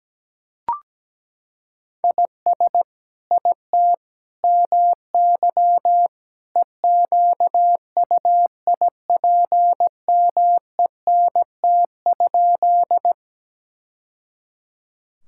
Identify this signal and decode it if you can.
Morse code, heard as a keyed tone, about 17 wpm, 700 Hz. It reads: IS IT MY EQUIPMENT?
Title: Sentences from Top 1000 Words - Review Part 5 - 17wpm